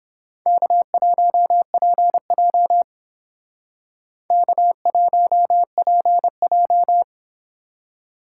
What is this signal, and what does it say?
Morse code, heard as a keyed tone, about 29 wpm, 700 Hz. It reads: K1PJ K1PJ